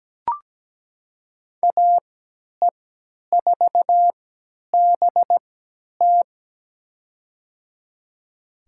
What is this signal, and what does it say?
Morse code, heard as a keyed tone, about 17 wpm, spaced as for 10 wpm, 700 Hz. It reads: AE4BT
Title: USA Call Signs - 5 Characters - 3x Character Spacing - 17wpm